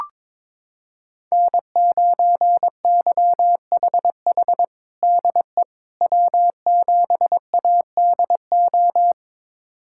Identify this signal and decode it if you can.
Morse code, heard as a keyed tone, about 22 wpm, 700 Hz. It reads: N9YHH DE W7ADO
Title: USA Callsign DE Callsign - 5 Characters - 22wpm